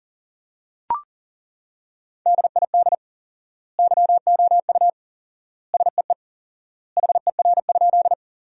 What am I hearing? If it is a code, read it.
Morse code, heard as a keyed tone, about 40 wpm, 700 Hz. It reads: DID YOU SEE HER?
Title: 4x Word-Spacing - Sentences from Top 100 Words - 40wpm